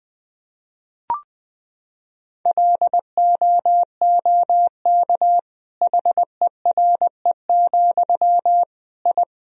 Morse code, heard as a keyed tone, about 20 wpm, 700 Hz. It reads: LOOK HERE, I
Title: Sentences from Top 1000 Words - Review Part 1 - 20wpm